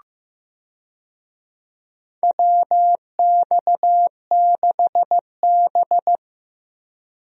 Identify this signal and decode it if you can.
Morse code, heard as a keyed tone, about 15 wpm, 700 Hz. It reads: WX6B
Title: USA Call Signs - 4 Characters - Repeated 2x - 15wpm